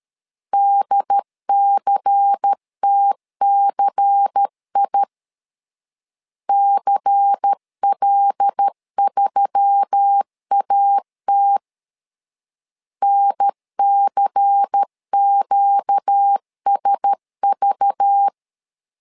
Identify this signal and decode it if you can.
Morse code, heard as a keyed tone, about 13 wpm, 800 Hz. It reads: DCTCI CL3AT NCQSV
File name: LEZIONI / 5 LBVC / 5 LBVC 12wpm 1955